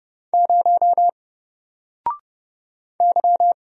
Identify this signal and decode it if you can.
Morse code, heard as a keyed tone, about 30 wpm, 700 Hz. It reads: 0 Y
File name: Single-Letter-Number-30wpm